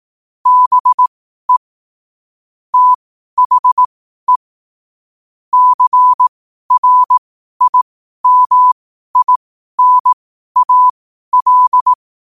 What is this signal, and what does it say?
Morse code, heard as a keyed tone, about 18 wpm, spaced as for 12 wpm, 1000 Hz. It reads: BE THE CRIMINAL
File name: qotdfiles / 12wpm